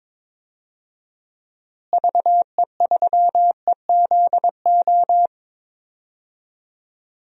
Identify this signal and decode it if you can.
Morse code, heard as a keyed tone, about 22 wpm, 700 Hz. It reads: VE3EZO
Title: DXCC-100 Call Signs - 6 Characters - Repeated 3x - 22wpm